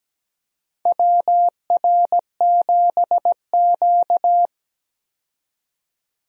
Morse code, heard as a keyed tone, about 17 wpm, 700 Hz. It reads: WR7Q
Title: USA Call Signs - 4 Characters - 17wpm